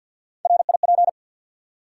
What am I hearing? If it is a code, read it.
Morse code, heard as a keyed tone, about 54 wpm, 700 Hz. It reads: RIP
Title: Three Letter Words - 50wpm